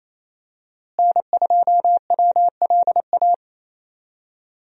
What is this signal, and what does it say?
Morse code, heard as a keyed tone, about 27 wpm, 700 Hz. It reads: N2WLA
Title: USA Call Signs - 5 Characters - Repeated 2x - 28wpm